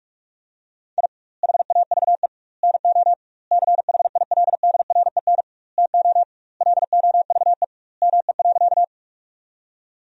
Morse code, heard as a keyed tone, about 46 wpm, 700 Hz. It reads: I HAVE NO CHILDREN TO LOVE ME.
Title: 2x Word-Spacing - Sentences from Top 400 Words - 45wpm